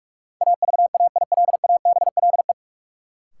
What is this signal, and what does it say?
Morse code, heard as a keyed tone, about 43 wpm, 700 Hz. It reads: AVAILABLE
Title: Sentences from Top 600 Words - New Word Drill - 45wpm